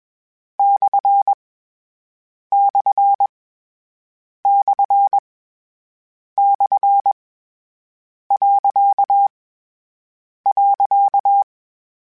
Morse code, heard as a keyed tone, about 21 wpm, 800 Hz. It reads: / / / / . .
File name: Misc / Misc-mp3s / Punct @20